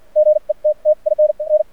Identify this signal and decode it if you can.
Morse code, heard as a keyed tone, about 57 wpm, spaced as for 42 wpm, 610 Hz. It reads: GETTIN N
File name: archived / CRSnet